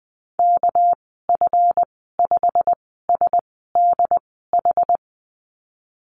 Morse code, heard as a keyed tone, about 20 wpm, spaced as for 15 wpm, 700 Hz. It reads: KF5SDH
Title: 200 Lesson 32g - Letter k - Call Signs 20wpm